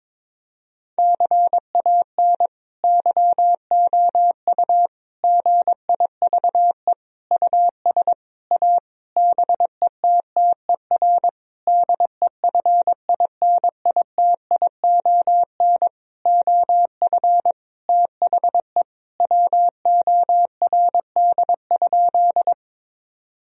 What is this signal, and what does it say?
Morse code, heard as a keyed tone, about 22 wpm, 700 Hz. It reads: CAN YOU GIVE US A BETTER DEFINITION OF THE WORD?
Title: Sentences from Top 1000 Words - Review Part 4 - 22wpm